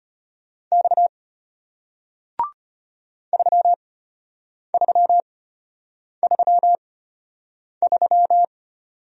Morse code, heard as a keyed tone, about 34 wpm, spaced as for 8 wpm, 700 Hz. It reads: X 3333